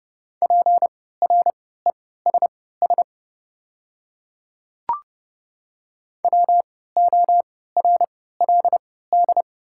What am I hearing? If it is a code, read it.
Morse code, heard as a keyed tone, about 30 wpm, spaced as for 17 wpm, 700 Hz. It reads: PRESS WORLD
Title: 3x Character Spacing - Top 500 Words - 30wpm